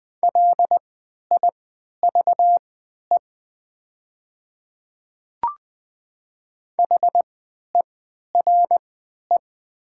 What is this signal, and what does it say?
Morse code, heard as a keyed tone, about 20 wpm, spaced as for 11 wpm, 700 Hz. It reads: LIVE HERE
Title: 3x Character Spacing - Top 200 Words - 20wpm